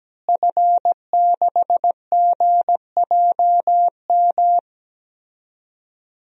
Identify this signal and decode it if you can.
Morse code, heard as a keyed tone, about 17 wpm, 700 Hz. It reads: F6GJM
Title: DXCC-100 Call Signs - 5 Characters - Repeated 3x - 17wpm